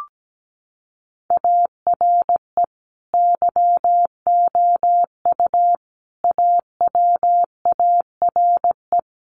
Morse code, heard as a keyed tone, about 17 wpm, 700 Hz. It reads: ARE YOU AWARE
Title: Sentences from Top 1200 Words 17wpm